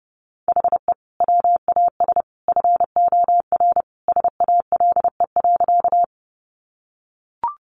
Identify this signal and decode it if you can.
Morse code, heard as a keyed tone, about 30 wpm, 700 Hz. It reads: HE WAS FOR SALE.
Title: Sentences from Top 1400 Words 30wpm